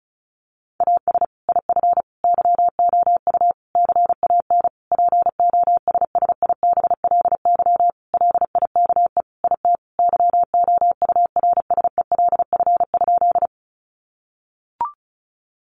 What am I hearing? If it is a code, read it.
Morse code, heard as a keyed tone, about 34 wpm, 700 Hz. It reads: AS IF YOU CAN POSSIBLY LIKE IT YOURSELF?
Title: Sentences from Top 1300 Words 35wpm